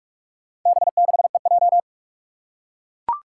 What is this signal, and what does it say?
Morse code, heard as a keyed tone, about 44 wpm, 700 Hz. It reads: B6EJ
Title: Four Challenging Letter-Number Combinations 45wpm